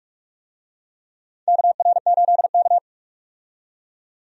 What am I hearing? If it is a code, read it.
Morse code, heard as a keyed tone, about 46 wpm, 700 Hz. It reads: KR8K
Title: USA Call Signs - 4 Characters - 45wpm